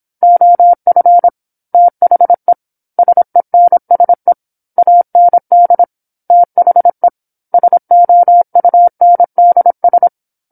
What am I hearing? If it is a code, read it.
Morse code, heard as a keyed tone, about 26 wpm, 700 Hz. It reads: OF THE SENSE AND THE SOUNDS
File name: 08-Chapter-08 26 WPM